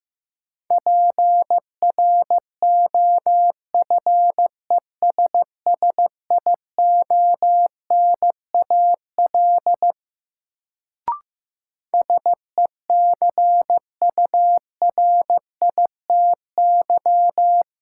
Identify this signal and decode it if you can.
Morse code, heard as a keyed tone, about 15 wpm, 700 Hz. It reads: PROFESSIONAL SECURITY
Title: Sentences from Top 1000 Words - Review Part 4 - New Word Drill - 15wpm